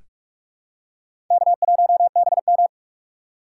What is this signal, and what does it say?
Morse code, heard as a keyed tone, about 43 wpm, 700 Hz. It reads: K1BM
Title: USA Call Signs - 4 Characters - 3x Character Spacing - 45wpm